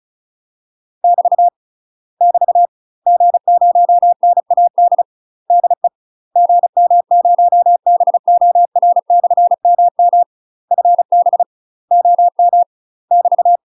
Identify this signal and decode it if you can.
Morse code, heard as a keyed tone, about 34 wpm, 700 Hz. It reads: <BT> <BT> G0NAD DE GM0BOR/MM FB OM <BT>
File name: UK MASTER QSOs TRAINING / 35WPM / UK MASTER QSOs 0037